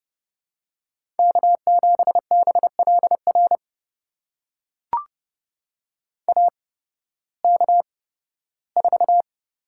Morse code, heard as a keyed tone, about 30 wpm, spaced as for 16 wpm, 700 Hz. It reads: K7BLR AK4
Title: USA Call Signs - 5 Characters - 8x Character Spacing - 30wpm